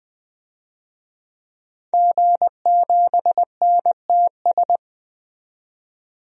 Morse code, heard as a keyed tone, about 20 wpm, 700 Hz. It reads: G7NTS
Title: DXCC-100 Call Signs - 5 Characters - 20wpm